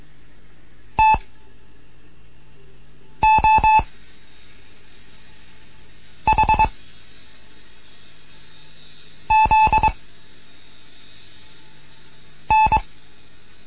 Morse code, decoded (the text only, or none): T O H Z N